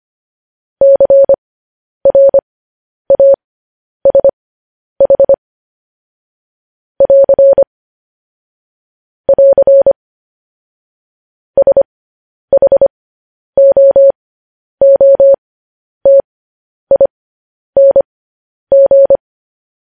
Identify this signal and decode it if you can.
Morse code, heard as a keyed tone, about 25 wpm, spaced as for 10 wpm, 550 Hz. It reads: CRASH <AR> <AR> SHOOTING
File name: podcast / News headlines-1x-25x10